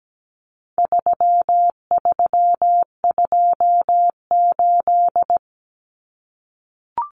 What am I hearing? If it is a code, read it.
Morse code, heard as a keyed tone, about 17 wpm, 700 Hz. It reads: 3328
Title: Common Cut Numbers - 4-Digit 17wpm